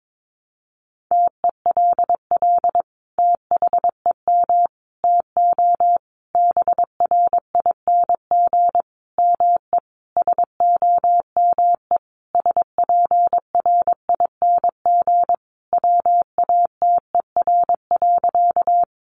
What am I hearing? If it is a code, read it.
Morse code, heard as a keyed tone, about 22 wpm, 700 Hz. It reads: TELL THEM TO BRING ME SOME SPRING WATER.
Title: Sentences from Top 1200 Words 22wpm